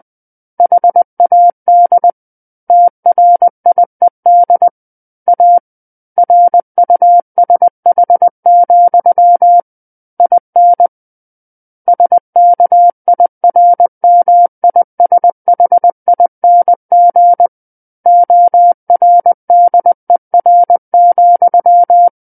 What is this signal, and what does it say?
Morse code, heard as a keyed tone, about 20 wpm, 700 Hz. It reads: HAD TRIED A RUSH, IN SKIRMISHING ORDER,
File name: 20 WPM